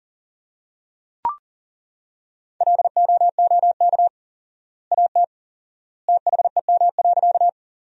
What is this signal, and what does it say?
Morse code, heard as a keyed tone, about 40 wpm, 700 Hz. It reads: LOOK AT THEM.
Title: 4x Word-Spacing - Sentences from Top 100 Words - 40wpm